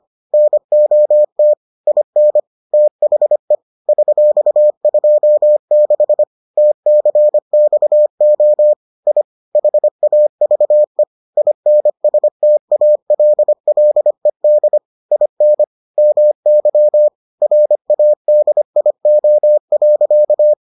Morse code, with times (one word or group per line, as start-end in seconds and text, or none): 0.3-1.5 NOT
1.9-2.4 IN
2.7-3.5 THE
3.9-6.2 $26
6.6-8.7 TCXO
9.1-9.2 I
9.6-11.0 HAVE
11.4-14.8 INSTALLED
15.1-15.6 IN
16.0-17.1 MY
17.4-20.5 RADIO.